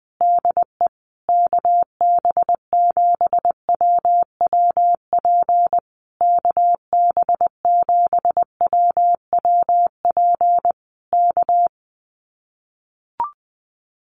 Morse code, heard as a keyed tone, about 20 wpm, 700 Hz. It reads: DE KB7WWP KB7WWP K
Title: Common QSO Phrases Encore 20wpm